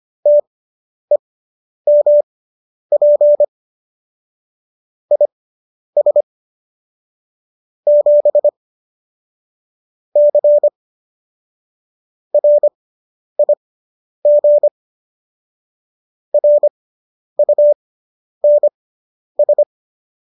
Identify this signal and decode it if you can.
Morse code, heard as a keyed tone, about 25 wpm, spaced as for 10 wpm, 590 Hz. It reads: TEMP IS 7 C RIG RUNS